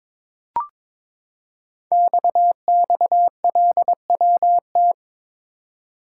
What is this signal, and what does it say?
Morse code, heard as a keyed tone, about 23 wpm, 700 Hz. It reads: XXLWT
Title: Code Groups - 5 Letters - 22wpm